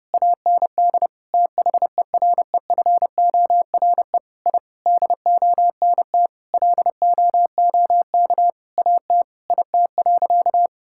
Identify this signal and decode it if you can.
Morse code, heard as a keyed tone, about 30 wpm, 700 Hz. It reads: AND THEREFORE I DONT LOOK AT IT.
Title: Sentences from Top 1000 Words - Review Part 5 - 30wpm